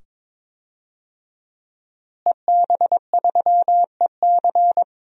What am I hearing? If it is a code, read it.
Morse code, heard as a keyed tone, about 21 wpm, 700 Hz. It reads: EB3EC
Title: DXCC-100 Call Signs - 5 Characters - Repeated 2x - 22wpm